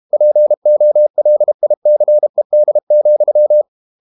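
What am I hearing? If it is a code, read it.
Morse code, heard as a keyed tone, about 32 wpm, 600 Hz. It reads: POLICED,
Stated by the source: Morse code practice at 32 wpm from CW med Gnister, reading Ken Follett's novel Never in English.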